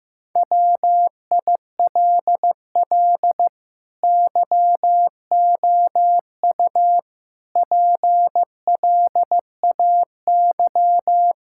Morse code, heard as a keyed tone, about 15 wpm, 700 Hz. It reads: WILL YOU PLAY
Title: Sentences from Top 1000 Words - Review Part 6 - 15wpm